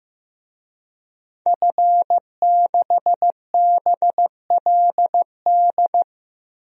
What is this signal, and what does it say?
Morse code, heard as a keyed tone, about 15 wpm, 700 Hz. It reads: F6BLD
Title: DXCC-100 Call Signs - 5 Characters - Repeated 2x - 15wpm